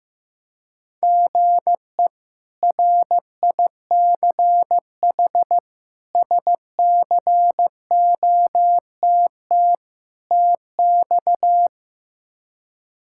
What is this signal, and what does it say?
Morse code, heard as a keyed tone, about 15 wpm, 700 Hz. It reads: GE RICH SCOTT TX
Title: K1USN SST QSO Practice - 15wpm